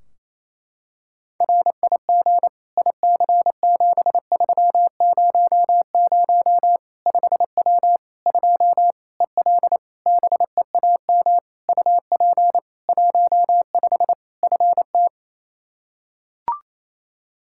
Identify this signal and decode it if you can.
Morse code, heard as a keyed tone, about 28 wpm, 700 Hz. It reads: RIG IC7300 5W 2 EL BEAM UP 15 FT
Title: K1USN SST QSO Practice 28wpm